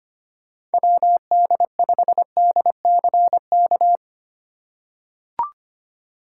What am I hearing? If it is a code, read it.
Morse code, heard as a keyed tone, about 25 wpm, 700 Hz. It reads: WD5DCK
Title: USA Call Signs - 6 Characters - 4x Character Spacing - 25wpm